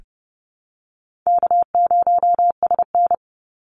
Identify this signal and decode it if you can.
Morse code, heard as a keyed tone, about 30 wpm, 700 Hz. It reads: K0SN
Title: Easy Call Signs Repeated 2x - 30wpm